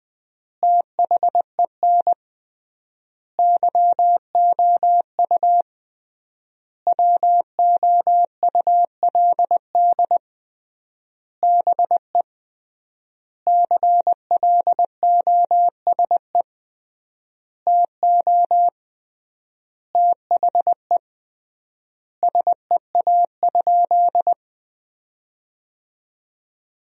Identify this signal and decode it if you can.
Morse code, heard as a keyed tone, about 20 wpm, 700 Hz. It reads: THEN YOU WOULD BE CLOSE TO THE SEA?
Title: 3x Word-Spacing - Sentences from Top 300 Words - 20wpm